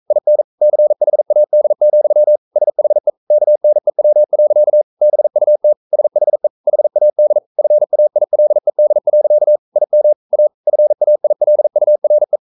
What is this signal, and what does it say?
Morse code, heard as a keyed tone, about 44 wpm, 600 Hz. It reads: IN CHAD, SHE KNEW. BUT SHE HAD FAILED. IM A FAILURE
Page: Romanen Never af Ken Follet på engelsk.